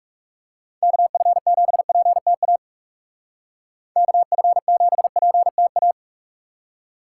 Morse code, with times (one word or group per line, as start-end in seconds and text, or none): 0.8-2.6 KF7PTA
4.0-5.9 KF7PTA